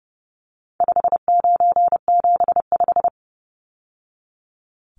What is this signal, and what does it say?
Morse code, heard as a keyed tone, about 30 wpm, 700 Hz. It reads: 5975
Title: Numbers - 4-Digit 30wpm